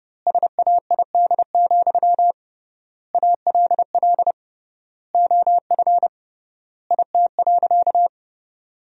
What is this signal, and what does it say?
Morse code, heard as a keyed tone, about 32 wpm, 700 Hz. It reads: SAID, ALL OF IT.
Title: 3x Word-Spacing - Sentences from Top 500 Words - 30wpm